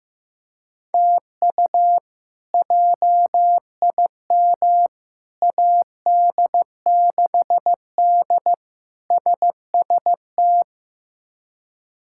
Morse code, heard as a keyed tone, about 15 wpm, 700 Hz. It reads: TU JIM AD6D SST
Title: K1USN SST QSO Practice - 15wpm